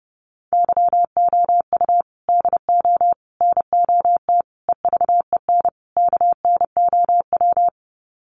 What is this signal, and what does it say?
Morse code, heard as a keyed tone, about 30 wpm, 700 Hz. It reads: YOU DO NOT EVEN KNOW
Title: Sets of 5 Words Encore2 30wpm